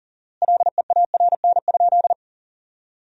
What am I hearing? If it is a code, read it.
Morse code, heard as a keyed tone, about 40 wpm, 700 Hz. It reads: LEARN?
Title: 4x Word-Spacing - Sentences from Top 300 Words - 40wpm